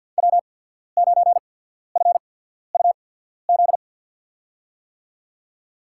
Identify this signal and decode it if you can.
Morse code, heard as a keyed tone, about 51 wpm, spaced as for 14 wpm, 700 Hz. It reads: W9FUZ
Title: USA Call Signs - 5 Characters - 8x Character Spacing - 50wpm